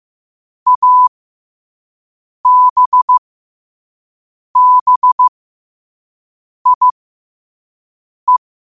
Morse code, heard as a keyed tone, about 15 wpm, spaced as for 5 wpm, 1000 Hz. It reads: ABBIE
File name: qotdfiles / 5wpm